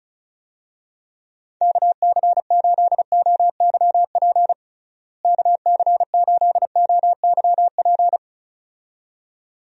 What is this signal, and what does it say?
Morse code, heard as a keyed tone, about 36 wpm, 700 Hz. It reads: KC8OYP KC8OYP
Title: USA Call Signs - 6 Characters - Repeated 2x - 35wpm